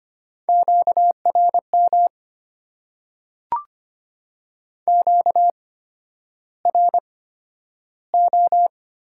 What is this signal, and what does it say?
Morse code, heard as a keyed tone, about 25 wpm, spaced as for 11 wpm, 700 Hz. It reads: QRM QRO
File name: Top 90 QSO Elements - 8x Character Spacing - 25wpm